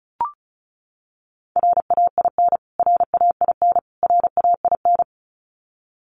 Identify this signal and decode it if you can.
Morse code, heard as a keyed tone, about 34 wpm, 700 Hz. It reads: RAIN RAIN RAIN
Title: Top 500 Words Repeated 35wpm